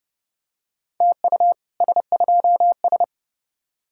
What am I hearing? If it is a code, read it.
Morse code, heard as a keyed tone, about 30 wpm, 700 Hz. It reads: TU S2S